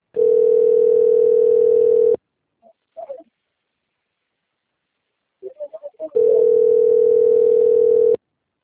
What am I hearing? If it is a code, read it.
Ringback tone (North America), 440 + 480 Hz.